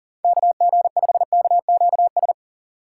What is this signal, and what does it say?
Morse code, heard as a keyed tone, about 40 wpm, 700 Hz. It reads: KG5KQS
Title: USA Call Signs - 4x Character Spacing - 40wpm